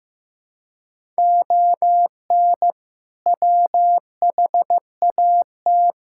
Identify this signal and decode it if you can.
Morse code, heard as a keyed tone, about 16 wpm, 700 Hz. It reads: ON WHAT
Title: Sentences from Top 1000 Words - Review Part 5 - 15wpm